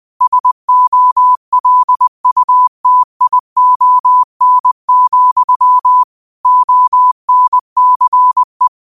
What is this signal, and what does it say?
Morse code, heard as a keyed tone, about 20 wpm, 1000 Hz. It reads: SOLUTION, ONCE
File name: qotdfiles / 20wpm